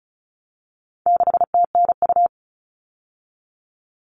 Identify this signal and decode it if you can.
Morse code, heard as a keyed tone, about 34 wpm, 700 Hz. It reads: 6TNU
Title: All Cut Numbers - 4-Digit 35wpm